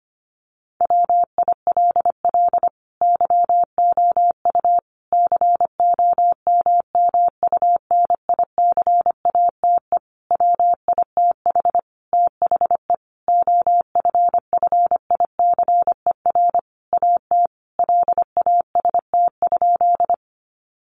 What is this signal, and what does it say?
Morse code, heard as a keyed tone, about 25 wpm, 700 Hz. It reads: WILL YOU COMMUNICATE WITH THE OFFICER AT LAST?
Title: Sentences from Top 1300 Words 25wpm